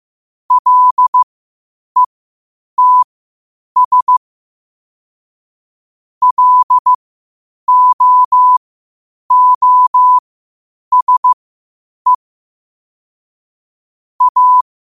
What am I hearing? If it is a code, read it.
Morse code, heard as a keyed tone, about 15 wpm, spaced as for 8 wpm, 1000 Hz. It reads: LETS LOOSE A
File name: qotdfiles / 8wpm